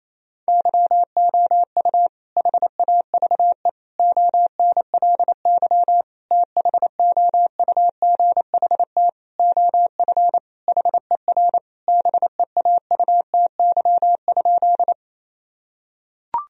Morse code, heard as a keyed tone, about 28 wpm, 700 Hz. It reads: YOU HAVE ONLY THOUGHT OF HER BEAUTY?
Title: Sentences from Top 1000 Words - Review Part 2 - 28wpm